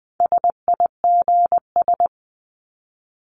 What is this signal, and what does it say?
Morse code, heard as a keyed tone, about 20 wpm, 700 Hz. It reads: SIGS
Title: Abbreviations 20wpm